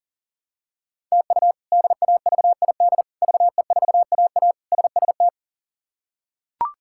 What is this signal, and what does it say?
Morse code, heard as a keyed tone, about 40 wpm, 700 Hz. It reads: TU DAVID VE4AA SST
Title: K1USN SST QSO Practice - 40wpm